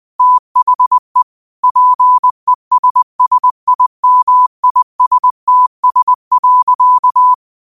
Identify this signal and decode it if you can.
Morse code, heard as a keyed tone, about 20 wpm, 1000 Hz. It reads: THE PESSIMISTS.
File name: qotdfiles / 20wpm